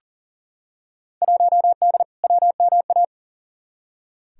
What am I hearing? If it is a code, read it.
Morse code, heard as a keyed tone, about 40 wpm, 700 Hz. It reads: 1D WMA
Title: Field Day Exchanges - 40wpm